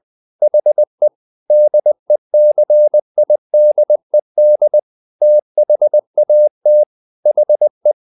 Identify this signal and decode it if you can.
Morse code, heard as a keyed tone, about 20 wpm, 600 Hz. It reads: HE DECIDED THAT HE
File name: the old man and the sea part 2 20wpm